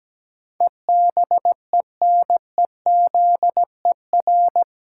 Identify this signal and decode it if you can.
Morse code, heard as a keyed tone, about 17 wpm, 700 Hz. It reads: EBENEZER
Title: Pangram Challenge v2 - 17wpm